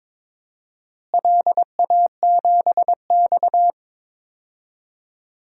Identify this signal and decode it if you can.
Morse code, heard as a keyed tone, about 22 wpm, 700 Hz. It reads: LA7X
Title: DXCC-100 Call Signs - 4 Characters - Repeated 3x - 22wpm